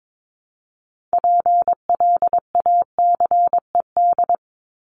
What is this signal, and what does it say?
Morse code, heard as a keyed tone, about 22 wpm, 700 Hz. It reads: PLACED
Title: ED Suffix - Top 100 Words with Base Words 22wpm